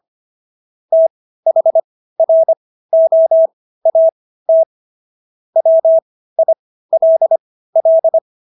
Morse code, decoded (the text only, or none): THROAT WILL